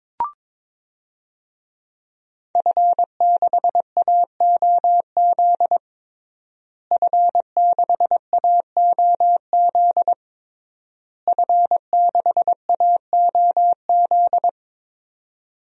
Morse code, heard as a keyed tone, about 22 wpm, 700 Hz. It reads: F6AOZ F6AOZ F6AOZ
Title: DXCC-100 Call Signs - 5 Characters - Repeated 3x - 22wpm